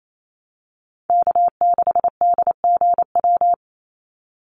Morse code, heard as a keyed tone, about 28 wpm, 700 Hz. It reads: K6DGW
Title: Call Signs Repeated 3x - 28wpm